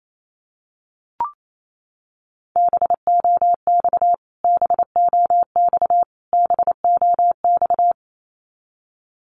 Morse code, heard as a keyed tone, about 28 wpm, 700 Hz. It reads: BOX BOX BOX
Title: Top 500 Words Repeated 28wpm